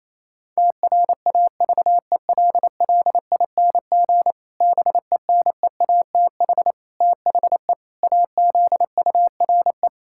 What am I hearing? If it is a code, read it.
Morse code, heard as a keyed tone, about 28 wpm, 700 Hz. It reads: TRAVELLING BENEATH THE AZURE
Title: Pangram Challenge v2 - 28wpm